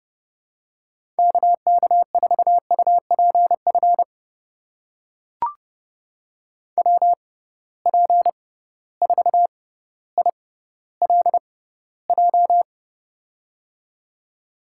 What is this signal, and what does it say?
Morse code, heard as a keyed tone, about 30 wpm, spaced as for 15 wpm, 700 Hz. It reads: KK4UPF WP4ILJ
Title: USA Call Signs - 6x Character Spacing - 30wpm